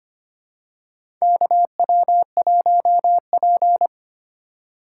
Morse code, heard as a keyed tone, about 25 wpm, 700 Hz. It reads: KW1P
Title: USA Call Signs - 2x Character Spacing - 25wpm